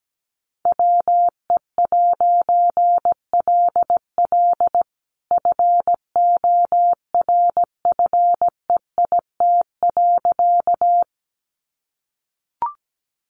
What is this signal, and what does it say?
Morse code, heard as a keyed tone, about 17 wpm, 700 Hz. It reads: WE'LL FORFEIT.
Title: A Sound Of Thunder 17wpm